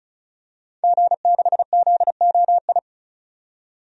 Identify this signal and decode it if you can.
Morse code, heard as a keyed tone, about 36 wpm, 700 Hz. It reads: G6ZOI
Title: DXCC-100 Call Signs - 5 Characters - 35wpm